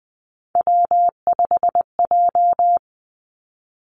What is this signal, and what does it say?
Morse code, heard as a keyed tone, about 20 wpm, 700 Hz. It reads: W5J
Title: Three Challenging Letter-Number Combinations 20wpm